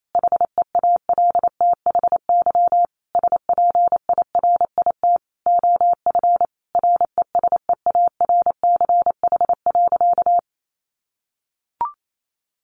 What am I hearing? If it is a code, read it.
Morse code, heard as a keyed tone, about 28 wpm, 700 Hz. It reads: HEALTHY SPIRIT OF RESEARCH.
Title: Sentences from Top 1400 Words 28wpm